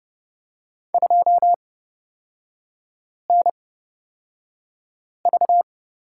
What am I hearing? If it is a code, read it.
Morse code, heard as a keyed tone, about 30 wpm, spaced as for 5 wpm, 700 Hz. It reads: 2NV
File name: Lesson 38e - Letter q - Three Characters 30wpm